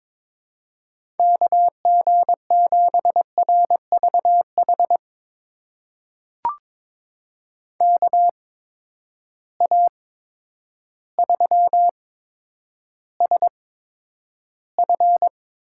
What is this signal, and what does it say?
Morse code, heard as a keyed tone, about 22 wpm, spaced as for 10 wpm, 700 Hz. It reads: KG7RVH KA3SF
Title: USA Call Signs - 6 Characters - 8x Character Spacing - 22wpm